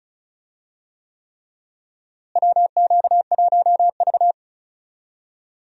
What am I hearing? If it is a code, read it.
Morse code, heard as a keyed tone, about 36 wpm, 700 Hz. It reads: WQ1V
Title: USA Call Signs - 4 Characters - Repeated 3x - 35wpm